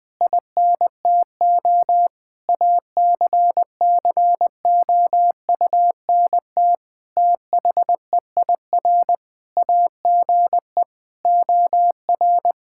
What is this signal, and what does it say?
Morse code, heard as a keyed tone, about 20 wpm, 700 Hz. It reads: INTO ACCOUNT THEIR AGE OR
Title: Sentences from Top 1600 Words 20wpm